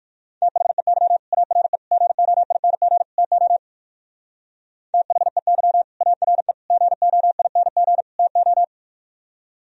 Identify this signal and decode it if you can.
Morse code, heard as a keyed tone, about 59 wpm, 700 Hz. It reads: THEY ARE GOING TO THEY ARE GOING TO